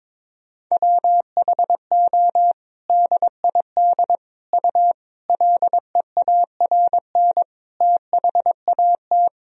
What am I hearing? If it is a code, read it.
Morse code, heard as a keyed tone, about 22 wpm, 700 Hz. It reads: WHO DID U LEARN THAT
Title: Sentences from Top 300 Words and Abbreviations 22wpm